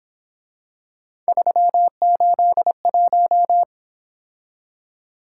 Morse code, heard as a keyed tone, about 26 wpm, 700 Hz. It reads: 381
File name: Speed-Racing - Three Letter-Number Combinations - 22wpm